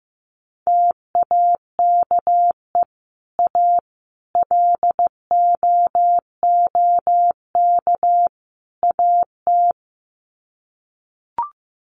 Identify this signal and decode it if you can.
Morse code, heard as a keyed tone, about 15 wpm, 700 Hz. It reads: TAKE A LOOK AT
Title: Sets of 4 Words 15wpm